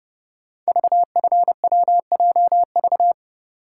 Morse code, heard as a keyed tone, about 30 wpm, 700 Hz. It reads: VFWJV
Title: Code Groups - 5 Letters - 30wpm